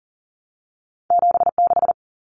Morse code, heard as a keyed tone, about 40 wpm, 700 Hz. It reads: 76
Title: Common Cut Numbers - 2-Digit 40wpm